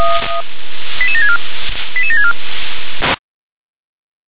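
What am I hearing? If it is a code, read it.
Five-tone selective call (ZVEI), heard as quick runs of five tones, tones of about 80 ms. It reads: 88764 88764
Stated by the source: ZVEI five-tone selective call 88775 received on SDR.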